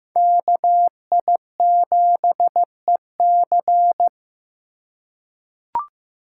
Morse code, heard as a keyed tone, about 15 wpm, 700 Hz. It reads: KI7EC
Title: USA Call Signs - 5 Characters - 4x Character Spacing - 15wpm